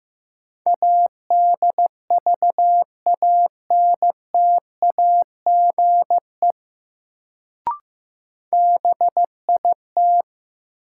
Morse code, heard as a keyed tone, about 15 wpm, 700 Hz. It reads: ADVANTAGE BIT
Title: Sentences from Top 900 Words - New Word Drill - 15wpm